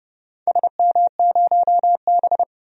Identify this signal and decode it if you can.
Morse code, heard as a keyed tone, about 30 wpm, 700 Hz. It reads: SM0B
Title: DXCC-100 Call Signs - 4 Characters - 30wpm